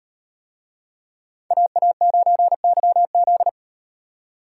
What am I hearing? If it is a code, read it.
Morse code, heard as a keyed tone, about 38 wpm, 700 Hz. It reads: AA9YZ